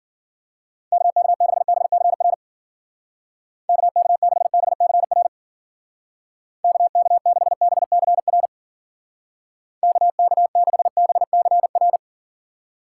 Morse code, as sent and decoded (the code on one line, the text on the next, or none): -.- -.- -.... -... -.-. .-. / -.- -.- -.... -... -.-. .-. / -.- -.- -.... -... -.-. .-. / -.- -.- -.... -... -.-. .-.
KK6BCR KK6BCR KK6BCR KK6BCR